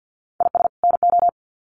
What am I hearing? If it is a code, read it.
Morse code, heard as a keyed tone, about 48 wpm, 700 Hz. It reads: IS NO
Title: Sets of 2 Words 50wpm